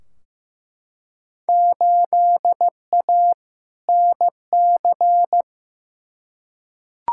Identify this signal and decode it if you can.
Morse code, heard as a keyed tone, about 16 wpm, 700 Hz. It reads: 8A NC
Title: Field Day Exchanges - 15wpm